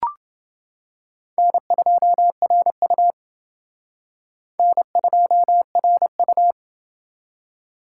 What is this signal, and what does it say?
Morse code, heard as a keyed tone, about 29 wpm, 700 Hz. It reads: N2RU N2RU